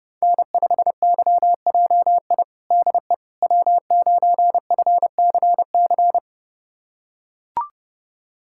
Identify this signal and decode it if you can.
Morse code, heard as a keyed tone, about 30 wpm, 700 Hz. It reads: N5YJI DE W9FCC
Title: USA Callsign DE Callsign - 5 Characters - 30wpm